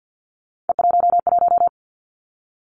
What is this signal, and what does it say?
Morse code, heard as a keyed tone, about 50 wpm, 700 Hz. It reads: E11
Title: Three Challenging Letter-Number Combinations 50wpm